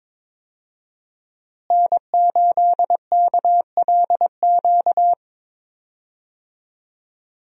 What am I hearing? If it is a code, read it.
Morse code, heard as a keyed tone, about 22 wpm, 690 Hz. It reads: N8KLQ
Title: USA Call Signs - 5 Characters - Repeated 3x - 22wpm